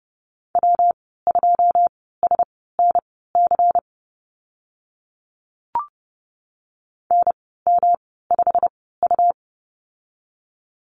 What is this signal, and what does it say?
Morse code, heard as a keyed tone, about 30 wpm, spaced as for 17 wpm, 700 Hz. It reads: W2SNC NM5U
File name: Lesson 19g - Letter w - Call Signs 30wpm